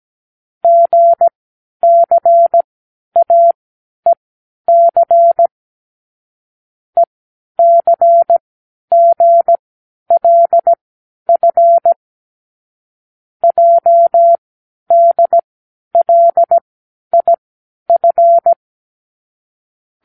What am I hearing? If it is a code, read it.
Morse code, heard as a keyed tone, about 17 wpm, spaced as for 10 wpm, 680 Hz. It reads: GCAEC ECGLF JDLIF
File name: k7qo / Lesson18(A-L)